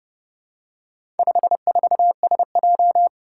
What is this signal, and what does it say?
Morse code, heard as a keyed tone, about 28 wpm, 700 Hz. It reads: 54SJ
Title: Four Challenging Letter-Number Combinations 30wpm